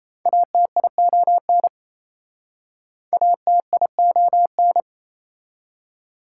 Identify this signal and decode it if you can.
Morse code, heard as a keyed tone, about 37 wpm, 700 Hz. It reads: ATION ATION